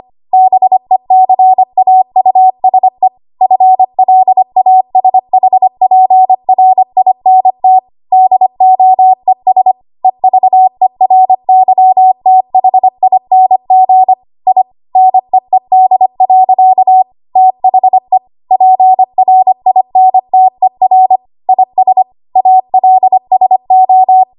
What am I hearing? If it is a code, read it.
Morse code, heard as a keyed tone, about 25 wpm, 750 Hz. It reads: BECAUSE FLASHPRINT DOES EVERYTHING I NEED. THE PRINTER IS ALSO
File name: Archive / 25 WPM / 25WPM